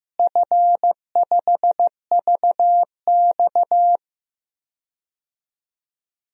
Morse code, heard as a keyed tone, about 16 wpm, 700 Hz. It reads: F5VX
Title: DXCC-100 Call Signs - 4 Characters - Repeated 3x - 15wpm